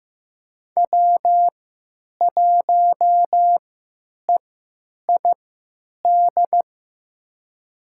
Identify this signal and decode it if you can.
Morse code, heard as a keyed tone, about 15 wpm, spaced as for 9 wpm, 700 Hz. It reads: W1EID
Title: USA Call Signs - 5 Characters - 3x Character Spacing - 15wpm